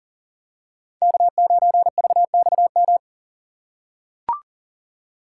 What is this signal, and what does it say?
Morse code, heard as a keyed tone, about 40 wpm, 700 Hz. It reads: K9VXM
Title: USA Call Signs - 5 Characters - 6x Character Spacing - 40wpm